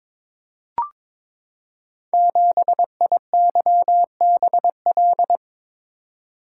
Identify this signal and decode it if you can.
Morse code, heard as a keyed tone, about 22 wpm, 700 Hz. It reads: 7IYBL
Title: Code Groups - 5 Letter:Numbers - 22wpm